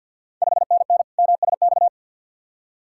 Morse code, heard as a keyed tone, about 48 wpm, 700 Hz. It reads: 5NN MSX
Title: Ontario QSO Party Exchange 50wpm